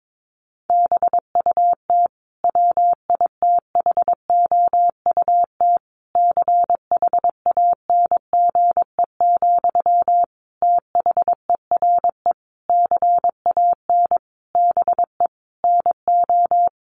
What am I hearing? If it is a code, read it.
Morse code, heard as a keyed tone, about 22 wpm, 700 Hz. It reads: BUT WITHOUT CHANGE, THERE CAN BE NO
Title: Sentences from Top 1400 Words 22wpm